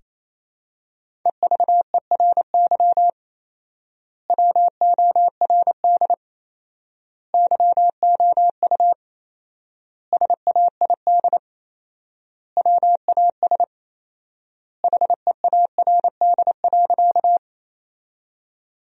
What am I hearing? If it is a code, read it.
Morse code, heard as a keyed tone, about 28 wpm, 700 Hz. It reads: EVERY WORD YOU SAID WAS HEARD.
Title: 4x Word-Spacing - Sentences from Top 400 Words - 28wpm